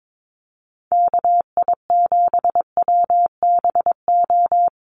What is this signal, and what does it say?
Morse code, heard as a keyed tone, about 22 wpm, 700 Hz. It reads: KI7WBO
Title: USA Call Signs 22wpm